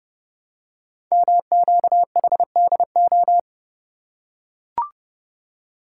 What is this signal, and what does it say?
Morse code, heard as a keyed tone, about 30 wpm, 700 Hz. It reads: MQHDO
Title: Code Groups - 5 Letters - 30wpm